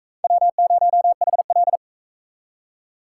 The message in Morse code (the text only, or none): W0HL